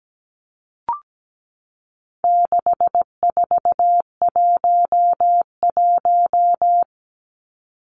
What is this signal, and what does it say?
Morse code, heard as a keyed tone, about 17 wpm, 700 Hz. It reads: 6411
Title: Numbers - 4-Digit 17wpm